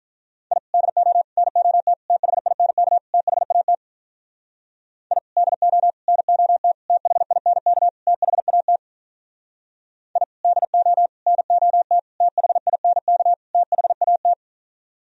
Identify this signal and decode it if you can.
Morse code, heard as a keyed tone, about 44 wpm, 700 Hz. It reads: I DO NOT THINK THAT I DO NOT THINK THAT I DO NOT THINK THAT